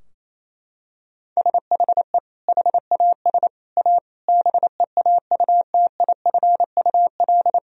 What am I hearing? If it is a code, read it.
Morse code, heard as a keyed tone, about 28 wpm, 700 Hz. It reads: SHE HAS A BEAUTIFUL
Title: Sentences from Top 1000 Words - Review Part 2 - 28wpm